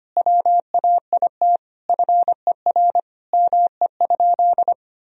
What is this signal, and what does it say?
Morse code, heard as a keyed tone, about 25 wpm, 700 Hz. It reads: WAIT FER ME?
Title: Sentences from Top 500 Words and Abbreviations 25wpm